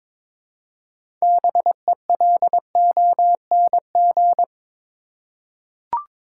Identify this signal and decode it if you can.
Morse code, heard as a keyed tone, about 21 wpm, 700 Hz. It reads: BELONG
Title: Six Letter Words - 22wpm